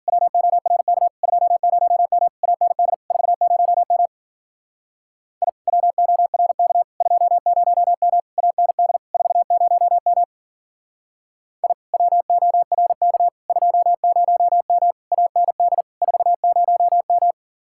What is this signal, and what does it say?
Morse code, heard as a keyed tone, about 50 wpm, 700 Hz. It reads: WORK 20M AND 40M I WORK 20M AND 40M I WORK 20M AND 40M
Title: Speed-Racing Common QSO Phrases Encore - 40wpm